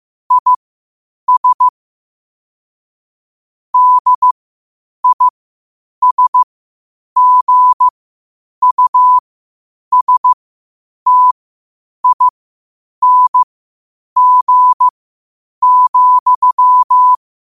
Morse code, heard as a keyed tone, about 15 wpm, spaced as for 8 wpm, 1000 Hz. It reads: IS DISGUSTING,
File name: qotdfiles / 8wpm